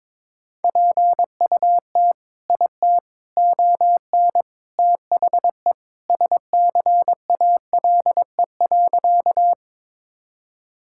Morse code, heard as a keyed tone, about 22 wpm, 700 Hz. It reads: PUT IT ON THE SCALE.
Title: Sentences from Top 1000 Words - Review Part 5 - 22wpm